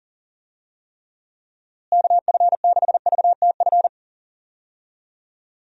Morse code, heard as a keyed tone, about 40 wpm, 700 Hz. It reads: KF6VTF
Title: USA Call Signs - 6 Characters - Repeated 3x - 40wpm